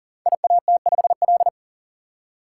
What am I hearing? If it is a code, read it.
Morse code, heard as a keyed tone, about 39 wpm, 700 Hz. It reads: IAT5L
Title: Code Groups - 5 Letter:Numbers - 40wpm